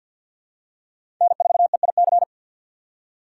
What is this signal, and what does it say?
Morse code, heard as a keyed tone, about 52 wpm, 700 Hz. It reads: N4EIC
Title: USA Call Signs - 5 Characters - 2x Character Spacing - 50wpm